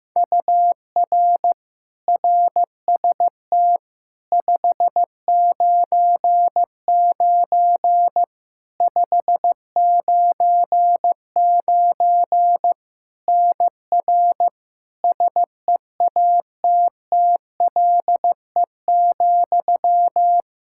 Morse code, heard as a keyed tone, about 15 wpm, 700 Hz. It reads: UR RST 599 599 NR SEATTLE,
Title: Practice QSO - Exchange 1 - Beginning - 15wpm